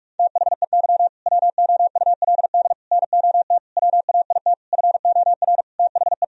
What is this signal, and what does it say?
Morse code, heard as a keyed tone, about 42 wpm, 700 Hz. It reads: THEY WOULD NOT WAIT FOR THE
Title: Sentences from Top 1000 Words - Review Part 3 - 45wpm